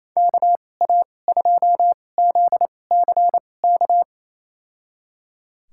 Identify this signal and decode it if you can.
Morse code, heard as a keyed tone, about 28 wpm, spaced as for 20 wpm, 700 Hz. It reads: KA2ZCK
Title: USA Call Signs - 6 Characters - 2x Character Spacing - 28wpm